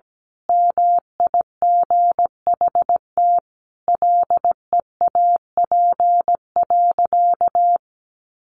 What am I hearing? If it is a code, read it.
Morse code, heard as a keyed tone, about 17 wpm, 700 Hz. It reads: MIGHT LEAP.
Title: A Sound Of Thunder 17wpm